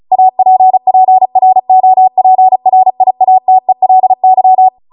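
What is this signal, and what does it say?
Morse code, heard as a keyed tone, about 36 wpm, 750 Hz. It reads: APPROPRIATELY
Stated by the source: Amateur radio Morse code practice archive transmission at 35 wpm.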